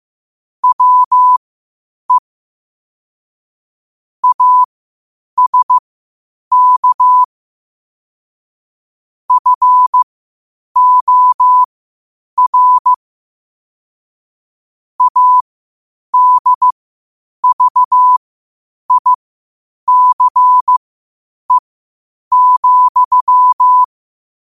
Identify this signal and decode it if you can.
Morse code, heard as a keyed tone, about 15 wpm, spaced as for 8 wpm, 1000 Hz. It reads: WE ASK FOR ADVICE,
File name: qotdfiles / 8wpm